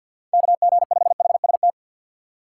Morse code, heard as a keyed tone, about 64 wpm, 700 Hz. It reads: KG5HST